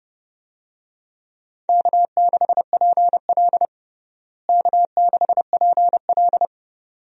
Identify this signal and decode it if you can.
Morse code, heard as a keyed tone, about 30 wpm, 700 Hz. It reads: K6PL K6PL
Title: USA Call Signs - 4 Characters - Repeated 2x - 30wpm